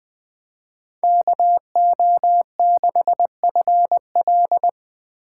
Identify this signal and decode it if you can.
Morse code, heard as a keyed tone, about 24 wpm, 700 Hz. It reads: KO6FL
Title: Speed Racing - USA 2x2 Call Signs 20wpm